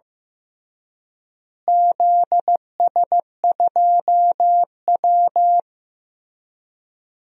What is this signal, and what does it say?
Morse code, heard as a keyed tone, about 15 wpm, 700 Hz. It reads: ZS2W
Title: DXCC-100 Call Signs - 4 Characters - Repeated 3x - 15wpm